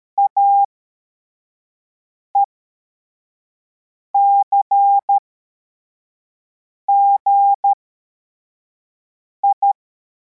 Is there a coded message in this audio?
Yes: Morse code from 0.2 to 9.7 s, heard as a keyed tone, about 13 wpm, spaced as for 4 wpm, 800 Hz. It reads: AECGI